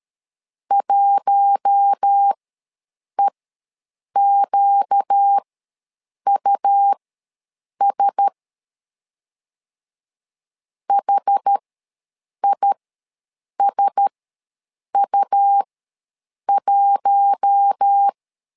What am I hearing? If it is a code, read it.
Morse code, heard as a keyed tone, about 13 wpm, spaced as for 7 wpm, 800 Hz. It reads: 1EQUS HISU1